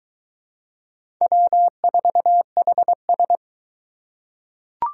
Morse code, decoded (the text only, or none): W4HS